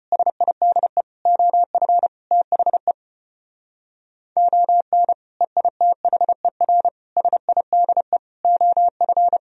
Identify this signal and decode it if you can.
Morse code, heard as a keyed tone, about 31 wpm, 700 Hz. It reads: SIDE OF THE ON EITHER SIDE OF